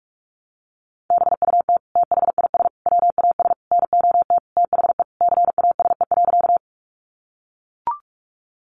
Morse code, heard as a keyed tone, about 44 wpm, 700 Hz. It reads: BUT THIS WAS NOT THE CASE.
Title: Sentences from Top 700 Words 45wpm